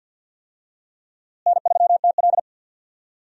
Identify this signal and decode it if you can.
Morse code, heard as a keyed tone, about 52 wpm, 700 Hz. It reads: N3TL
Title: USA Call Signs - 4 Characters - Repeated 2x - 50wpm